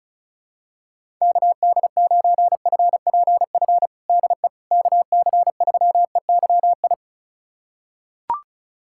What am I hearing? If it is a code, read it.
Morse code, heard as a keyed tone, about 36 wpm, 700 Hz. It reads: KD9FPF DE KC3EYI
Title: USA Callsign DE Callsign - 6 Characters - 35wpm